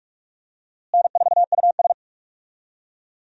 Morse code, heard as a keyed tone, about 46 wpm, 700 Hz. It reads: N4US
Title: USA Call Signs - 4 Characters - Repeated 3x - 45wpm